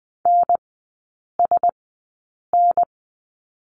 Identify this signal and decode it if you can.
Morse code, heard as a keyed tone, about 20 wpm, spaced as for 8 wpm, 700 Hz. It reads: NSN
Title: 041 Lesson 8e - Number 1 - Three Characters 20wpm